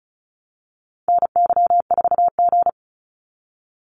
Morse code, heard as a keyed tone, about 34 wpm, 700 Hz. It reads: NY4G
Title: NA SOTA Activator Callsigns 35wpm